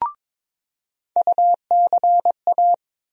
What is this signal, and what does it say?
Morse code, heard as a keyed tone, about 22 wpm, 700 Hz. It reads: UCA
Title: USA Call Sign Postfixes - 3 Letters 22wpm